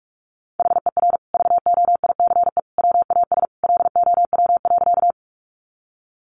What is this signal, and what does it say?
Morse code, heard as a keyed tone, about 44 wpm, 700 Hz. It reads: HER VOICE WAS LOW.
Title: Sentences from Top 500 Words 45wpm